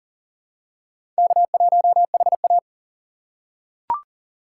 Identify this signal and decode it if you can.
Morse code, heard as a keyed tone, about 40 wpm, 700 Hz. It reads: K1HA
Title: USA Call Signs - 4 Characters - 4x Character Spacing - 40wpm